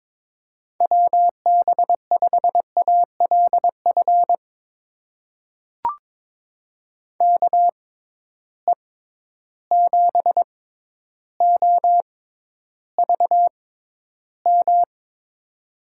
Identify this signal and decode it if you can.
Morse code, heard as a keyed tone, about 22 wpm, spaced as for 11 wpm, 700 Hz. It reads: WB5ALF KE7OVM
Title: USA Call Signs - 6 Characters - 6x Character Spacing - 22wpm